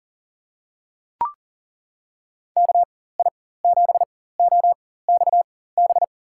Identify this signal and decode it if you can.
Morse code, heard as a keyed tone, about 37 wpm, spaced as for 18 wpm, 700 Hz. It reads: KI7OXB
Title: USA Call Signs - 4x Character Spacing - 40wpm